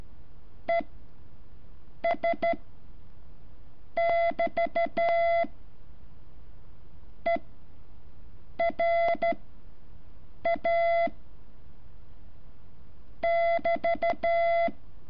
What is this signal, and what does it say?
Morse code, heard as a keyed tone, about 13 wpm, spaced as for 6 wpm, 680 Hz. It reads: ES <BT> ERA <BT>
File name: curso / ejercicio 14